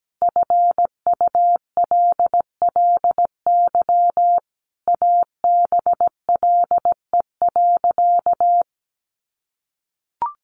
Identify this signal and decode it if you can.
Morse code, heard as a keyed tone, about 17 wpm, 700 Hz. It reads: FULLY ABLE.
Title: Sentences from Top 1300 Words 17wpm